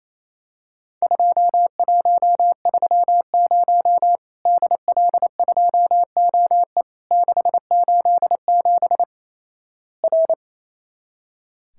Morse code, heard as a keyed tone, about 28 wpm, 700 Hz. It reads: 2130 DL2OE 687 R
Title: Worked All Europe DX Contest - QTC Exchange - 28wpm